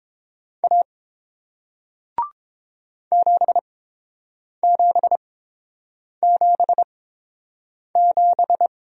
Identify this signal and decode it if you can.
Morse code, heard as a keyed tone, about 30 wpm, spaced as for 7 wpm, 700 Hz. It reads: A 7777